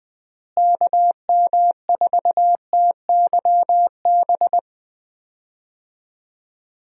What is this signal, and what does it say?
Morse code, heard as a keyed tone, about 20 wpm, 700 Hz. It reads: KM4TYB
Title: USA Call Signs - 6 Characters - Repeated 2x - 20wpm